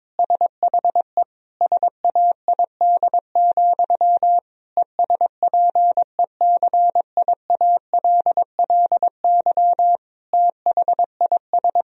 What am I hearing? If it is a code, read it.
Morse code, heard as a keyed tone, about 22 wpm, 700 Hz. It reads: SHE SAID, ESPECIALLY THIS
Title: Sentences from Top 1000 Words - Review Part 4 - 22wpm